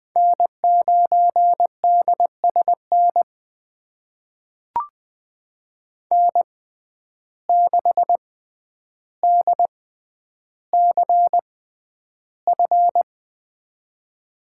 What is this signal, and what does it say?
Morse code, heard as a keyed tone, about 20 wpm, spaced as for 10 wpm, 700 Hz. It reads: N9DSN N6DCF
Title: USA Call Signs - 5 Characters - 6x Character Spacing - 20wpm